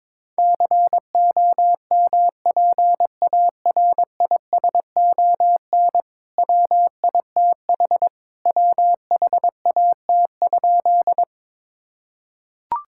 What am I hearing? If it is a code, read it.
Morse code, heard as a keyed tone, about 22 wpm, 700 Hz. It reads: COMPARISON WITH WHAT?
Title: Sentences from Top 1600 Words 22wpm